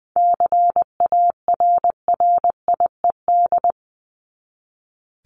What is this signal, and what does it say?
Morse code, heard as a keyed tone, about 20 wpm, 700 Hz. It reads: CARRIED
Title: ED Suffix - Top 100 Words with Base Words 20wpm